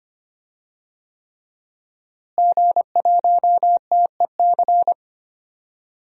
Morse code, heard as a keyed tone, about 25 wpm, 700 Hz. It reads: G1TEC
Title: DXCC-100 Call Signs - 5 Characters - Repeated 3x - 25wpm